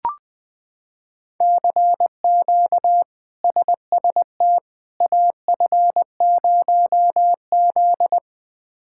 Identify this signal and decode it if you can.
Morse code, heard as a keyed tone, about 20 wpm, 700 Hz. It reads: CQ SST AF0Z
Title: K1USN SST QSO Practice - 20wpm